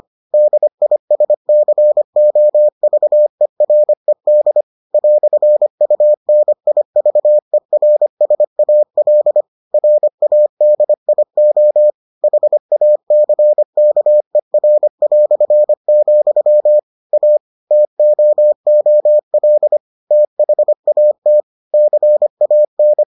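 Morse code, heard as a keyed tone, about 25 wpm, 600 Hz. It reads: DISCOVERED "UNIVERSAL RADIO HACKER", A TOOL THAT CAN
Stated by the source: Morse edition of the Foundations of Amateur Radio episodes.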